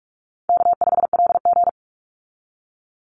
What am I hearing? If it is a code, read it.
Morse code, heard as a keyed tone, about 44 wpm, 700 Hz. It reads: K5LG
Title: Call Signs - 45wpm